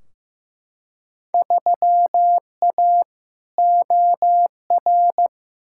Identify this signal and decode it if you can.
Morse code, heard as a keyed tone, about 15 wpm, 700 Hz. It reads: 3A OR
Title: Field Day Exchanges - 15wpm